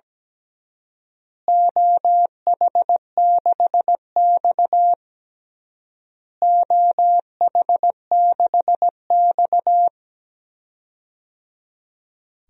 Morse code, heard as a keyed tone, about 17 wpm, 700 Hz. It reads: OH6X OH6X
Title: DXCC-100 Call Signs - 4 Characters - Repeated 3x - 17wpm